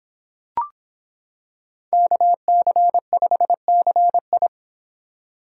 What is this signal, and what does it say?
Morse code, heard as a keyed tone, about 26 wpm, 700 Hz. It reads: KC5CI